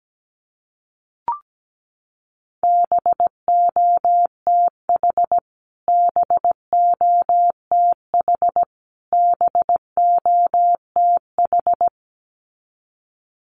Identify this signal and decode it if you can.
Morse code, heard as a keyed tone, about 17 wpm, 700 Hz. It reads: BOTH BOTH BOTH
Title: Top 300 Words Repeated 17wpm